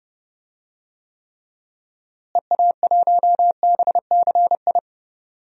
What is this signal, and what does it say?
Morse code, heard as a keyed tone, about 30 wpm, 700 Hz. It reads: EA1BCI
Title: DXCC-100 Call Signs - 6 Characters - Repeated 3x - 30wpm